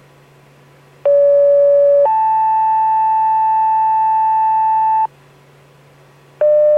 Two-tone page, heard as a one-second tone, then a three-second tone, about 570 then 880 Hz. It is heard from 1.0 to 5.1 s.